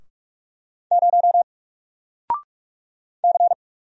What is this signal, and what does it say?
Morse code, heard as a keyed tone, about 45 wpm, 700 Hz. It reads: 0 C
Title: Single Letter-Number 45wpm